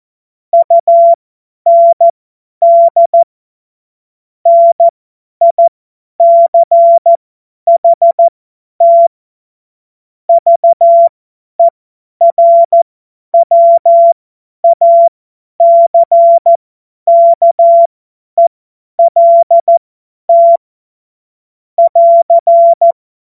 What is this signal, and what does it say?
Morse code, heard as a keyed tone, about 14 wpm, spaced as for 10 wpm, 680 Hz. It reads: UND NICHT VERWACKELT <AR>
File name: Witz-07-Oktober-2025-50-F70